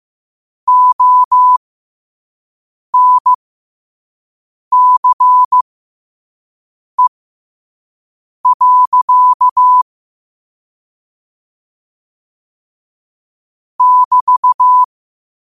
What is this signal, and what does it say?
Morse code, heard as a keyed tone, about 15 wpm, spaced as for 5 wpm, 1000 Hz. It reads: ONCE. <BT>
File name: qotdfiles / 5wpm